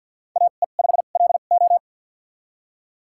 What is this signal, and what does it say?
Morse code, heard as a keyed tone, about 52 wpm, spaced as for 36 wpm, 700 Hz. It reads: AE5LO
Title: USA Call Signs - 5 Characters - 2x Character Spacing - 50wpm